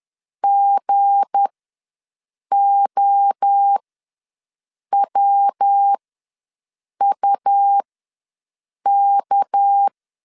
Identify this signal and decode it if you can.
Morse code, heard as a keyed tone, about 10 wpm, spaced as for 6 wpm, 800 Hz. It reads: GOWUK
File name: LEZIONI / 8 YJ45 / 8 YJ45 ws 10wpm 3843